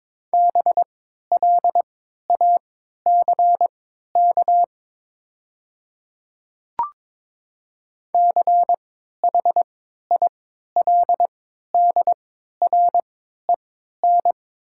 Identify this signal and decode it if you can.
Morse code, heard as a keyed tone, about 22 wpm, spaced as for 13 wpm, 690 Hz. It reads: BLACK CHILDREN
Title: 3x Character Spacing - Top 500 Words - 22wpm